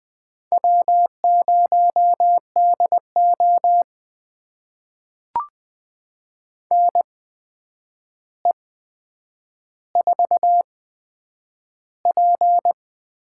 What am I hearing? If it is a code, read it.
Morse code, heard as a keyed tone, about 20 wpm, spaced as for 9 wpm, 700 Hz. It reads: W0DO NE4P
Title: USA Call Signs - 8x Character Spacing - 20wpm